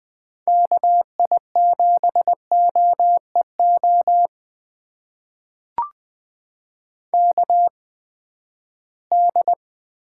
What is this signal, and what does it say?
Morse code, heard as a keyed tone, about 20 wpm, spaced as for 14 wpm, 700 Hz. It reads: KI7OEO KD
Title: USA Call Signs - 6 Characters - 8x Character Spacing - 20wpm